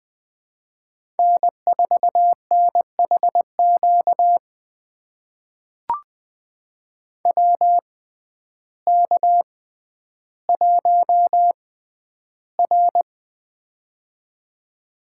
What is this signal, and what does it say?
Morse code, heard as a keyed tone, about 20 wpm, spaced as for 11 wpm, 700 Hz. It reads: N4NHQ WK1R
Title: USA Call Signs - 6x Character Spacing - 20wpm